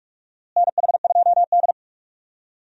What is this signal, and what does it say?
Morse code, heard as a keyed tone, about 46 wpm, 700 Hz. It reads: NH2D
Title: USA Call Signs - 4 Characters - Repeated 3x - 45wpm